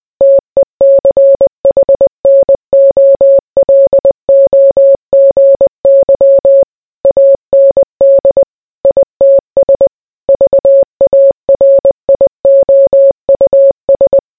Morse code, heard as a keyed tone, about 20 wpm, 550 Hz. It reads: TECHNOLOGY AND ITS VARIOUS